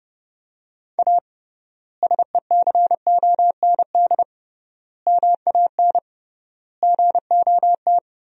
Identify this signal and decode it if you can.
Morse code, heard as a keyed tone, about 32 wpm, 700 Hz. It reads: A SECOND MAN GOT
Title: 3x Word-Spacing - Sentences from Top 300 Words - 30wpm